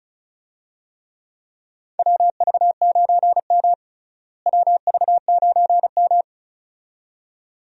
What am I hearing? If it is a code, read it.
Morse code, heard as a keyed tone, about 36 wpm, 700 Hz. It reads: WV9M WV9M
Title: USA Call Signs - 4 Characters - Repeated 2x - 35wpm